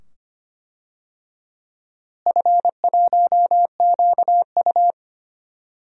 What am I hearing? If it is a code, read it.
Morse code, heard as a keyed tone, about 25 wpm, 700 Hz. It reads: F1QU
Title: DXCC-100 Call Signs - 4 Characters - Repeated 2x - 25wpm